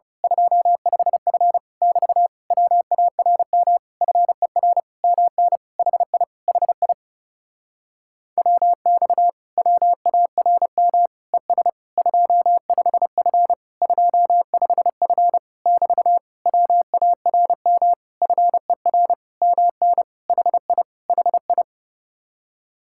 Morse code, heard as a keyed tone, about 29 wpm, 700 Hz. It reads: 25F <BT> WARM FER MN HI HI WX WARM ES 25F 25F <BT> WARM FER MN HI HI
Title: Speed-Racing Common QSO Phrases Encore - 30wpm